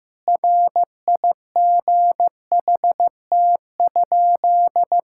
Morse code, heard as a keyed tone, about 16 wpm, 700 Hz. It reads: RIGHT?
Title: Sentences from Top 1000 Words - Review Part 5 - 15wpm